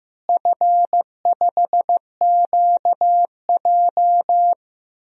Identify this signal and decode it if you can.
Morse code, heard as a keyed tone, about 15 wpm, 700 Hz. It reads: F5QJ
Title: DXCC-100 Call Signs - 4 Characters - Repeated 3x - 15wpm